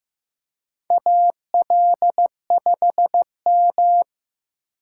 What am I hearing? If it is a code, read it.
Morse code, heard as a keyed tone, about 15 wpm, 700 Hz. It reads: AL5M
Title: USA Call Signs - 4 Characters - 2x Character Spacing - 15wpm